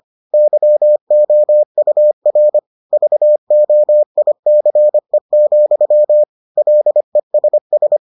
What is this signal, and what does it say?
Morse code, heard as a keyed tone, about 25 wpm, 600 Hz. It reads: YOUR VOICE, LESS